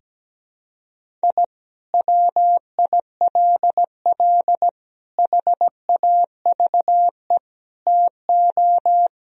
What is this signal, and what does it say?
Morse code, heard as a keyed tone, about 17 wpm, 700 Hz. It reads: I WILL HAVE TO
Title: Sentences from Top 1000 Words - Review Part 1 - 17wpm